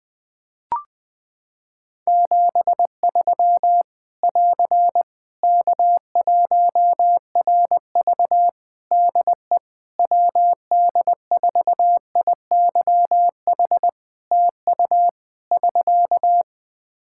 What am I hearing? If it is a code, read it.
Morse code, heard as a keyed tone, about 21 wpm, 700 Hz. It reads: 73 <AR> K1RV DE WD4IYH TU <SK>
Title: Practice QSO - Exchange 4 - Ending - 20wpm